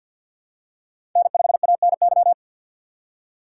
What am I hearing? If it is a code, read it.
Morse code, heard as a keyed tone, about 51 wpm, 700 Hz. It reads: N5ANY
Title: USA Call Signs - 5 Characters - 2x Character Spacing - 50wpm